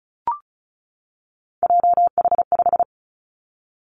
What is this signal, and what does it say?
Morse code, heard as a keyed tone, about 36 wpm, 700 Hz. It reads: JH5
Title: Three Challenging Letter-Number Combinations 35wpm